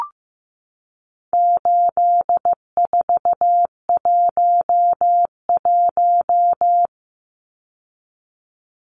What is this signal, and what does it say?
Morse code, heard as a keyed tone, about 15 wpm, 700 Hz. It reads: 8411
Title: Numbers - 4-Digit 15wpm